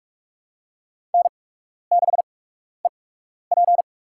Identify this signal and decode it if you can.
Morse code, heard as a keyed tone, about 46 wpm, spaced as for 12 wpm, 700 Hz. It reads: N6EP
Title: USA Call Signs - 4 Characters - 8x Character Spacing - 45wpm